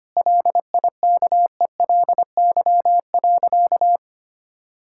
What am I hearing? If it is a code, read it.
Morse code, heard as a keyed tone, about 25 wpm, 700 Hz. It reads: LIKELY.
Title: Sentences from Top 1600 Words 25wpm